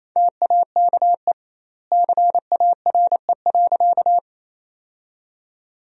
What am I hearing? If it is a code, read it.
Morse code, heard as a keyed tone, about 28 wpm, 700 Hz. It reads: TAKE CARE.
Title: 2x Word-Spacing - Sentences from Top 300 Words - 28wpm